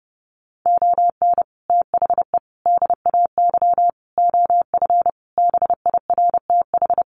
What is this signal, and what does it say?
Morse code, heard as a keyed tone, about 28 wpm, 700 Hz. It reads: ON THE DAY OF BIRTH
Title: Sentences from Top 1100 Words 30wpm